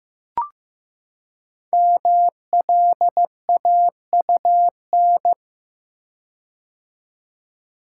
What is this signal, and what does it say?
Morse code, heard as a keyed tone, about 15 wpm, 700 Hz. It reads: MLAUN
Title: Code Groups - 5 Letters - 15wpm